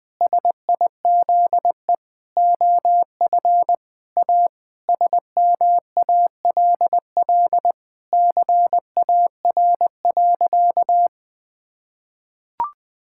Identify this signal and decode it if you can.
Morse code, heard as a keyed tone, about 20 wpm, 700 Hz. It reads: SIZE OF A SMALL CAR.
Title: Sentences with 4-Word Phrases - 20wpm